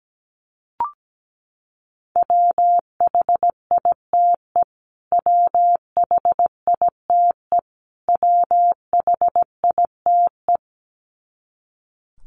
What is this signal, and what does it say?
Morse code, heard as a keyed tone, about 17 wpm, 700 Hz. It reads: WHITE WHITE WHITE
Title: Top 500 Words Repeated 17wpm